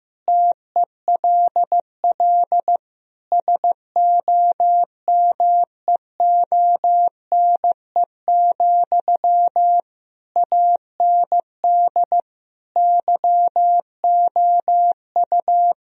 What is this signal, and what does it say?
Morse code, heard as a keyed tone, about 15 wpm, 700 Hz. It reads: TELL SOMEONE, AND YOU
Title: Sentences from Top 1000 Words - Review Part 4 - 15wpm